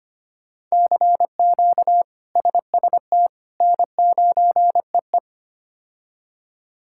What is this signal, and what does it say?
Morse code, heard as a keyed tone, about 25 wpm, 700 Hz. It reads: CQ SST N9EE
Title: K1USN SST QSO Practice - 25wpm